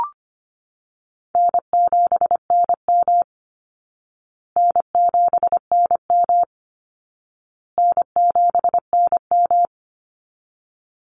Morse code, heard as a keyed tone, about 25 wpm, 700 Hz. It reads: N7NM N7NM N7NM
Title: Easy Call Signs Repeated 3x - 25wpm